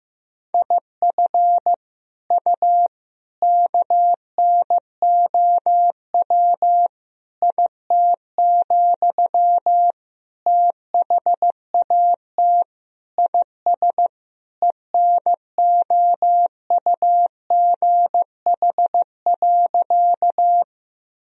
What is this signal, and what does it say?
Morse code, heard as a keyed tone, about 15 wpm, 700 Hz. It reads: IF U KNOW IT, THAT IS ENOUGH.
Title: Sentences from Top 400 Words and Abbreviations 15wpm